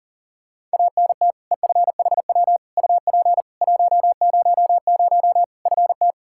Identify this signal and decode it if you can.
Morse code, heard as a keyed tone, about 40 wpm, 700 Hz. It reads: ANT EFHW UP 100 FT
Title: Practice QSO - Exchange 2 - Info - 40wpm